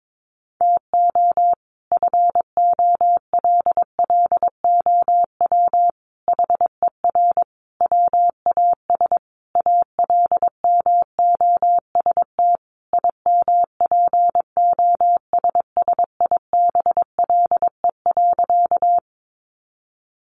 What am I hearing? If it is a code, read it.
Morse code, heard as a keyed tone, about 22 wpm, 700 Hz. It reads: TO FOLLOW HER WAS ALMOST IMPOSSIBLE.
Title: Sentences from Top 1100 Words 22wpm